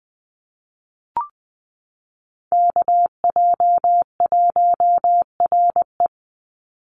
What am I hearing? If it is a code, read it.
Morse code, heard as a keyed tone, about 20 wpm, 700 Hz. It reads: KJ1RE
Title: Call Signs Repeated 3x - 20wpm